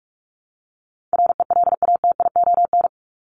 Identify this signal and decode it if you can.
Morse code, heard as a keyed tone, about 44 wpm, 700 Hz. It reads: RELATION
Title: Top 1751-2000 Words 45wpm